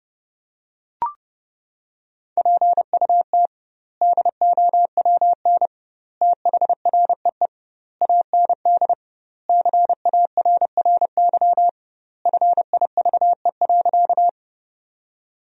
Morse code, heard as a keyed tone, about 30 wpm, 700 Hz. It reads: PUT DOWN THREE AND CARRY FIVE.
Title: 2x Word-Spacing - Sentences from Top 400 Words - 30wpm